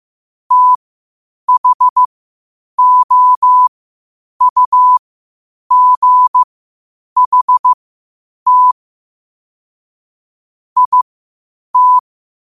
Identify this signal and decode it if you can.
Morse code, heard as a keyed tone, about 15 wpm, spaced as for 8 wpm, 1000 Hz. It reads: THOUGHT IT
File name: qotdfiles / 8wpm